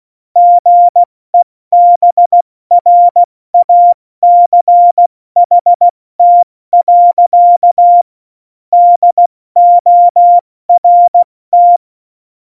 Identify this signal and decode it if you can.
Morse code, heard as a keyed tone, about 16 wpm, spaced as for 14 wpm, 700 Hz. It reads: GEBRACHT. DORT
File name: Morsekurs / Witze / Witz 26. Januar 2021, 70 F80